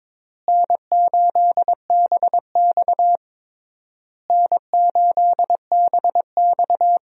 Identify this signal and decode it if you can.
Morse code, heard as a keyed tone, about 22 wpm, 700 Hz. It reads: N8BX N8BX
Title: USA Call Signs - 4 Characters - Repeated 3x - 22wpm